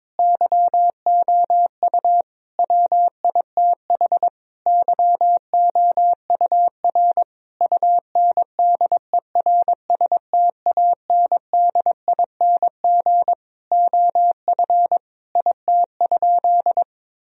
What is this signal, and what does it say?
Morse code, heard as a keyed tone, about 22 wpm, 700 Hz. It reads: YOU WITH YOUR UNDERSTANDING OF IT?
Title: Sentences from Top 1000 Words - Review Part 3 - 22wpm